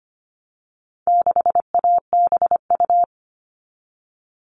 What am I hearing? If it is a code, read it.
Morse code, heard as a keyed tone, about 25 wpm, 700 Hz. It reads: 6ABU
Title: All Cut Numbers - 4-Digit 25wpm